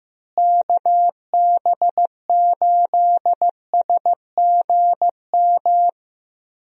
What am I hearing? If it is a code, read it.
Morse code, heard as a keyed tone, about 15 wpm, 700 Hz. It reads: KB8SGM
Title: USA Call Signs - 6 Characters - Repeated 3x - 15wpm